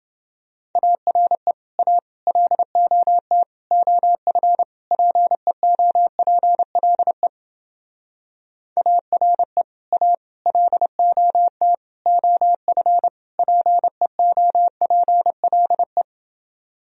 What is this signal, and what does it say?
Morse code, heard as a keyed tone, about 29 wpm, 700 Hz. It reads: ARE A LOT OF PEOPLE ARE A LOT OF PEOPLE